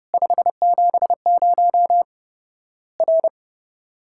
Morse code, heard as a keyed tone, about 30 wpm, 700 Hz. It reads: 570 R
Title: Worked All Europe DX Contest - QTC Exchange - 30wpm